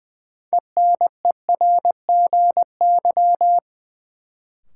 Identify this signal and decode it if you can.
Morse code, heard as a keyed tone, about 21 wpm, 700 Hz. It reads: ENERGY
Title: Sentences from Top 900 Words - New Word Drill - 20wpm